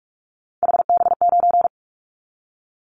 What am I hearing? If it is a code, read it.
Morse code, heard as a keyed tone, about 44 wpm, 700 Hz. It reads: HB9
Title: Callsign Prefixes for Top 25 DXCC - Repeated 3x - 45wpm